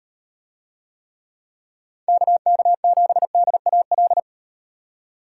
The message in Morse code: KK7DAL

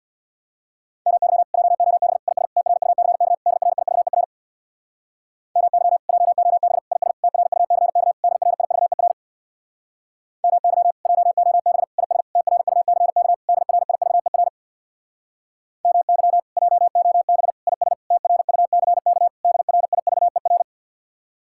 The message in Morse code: MY JOB IS TRUCK DRIVER MY JOB IS TRUCK DRIVER MY JOB IS TRUCK DRIVER MY JOB IS TRUCK DRIVER

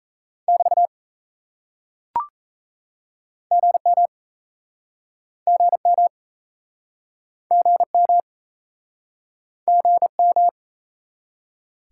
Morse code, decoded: <BT> GM GM GM GM

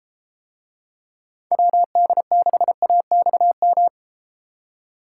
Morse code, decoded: WD6AXM

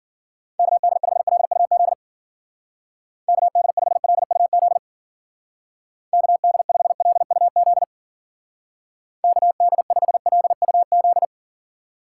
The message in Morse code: KD5LUZ KD5LUZ KD5LUZ KD5LUZ

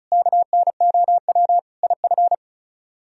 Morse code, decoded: KNOW IF

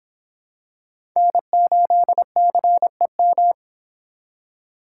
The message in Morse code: N8CEM

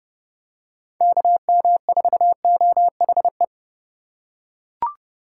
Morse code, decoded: KM4OHE

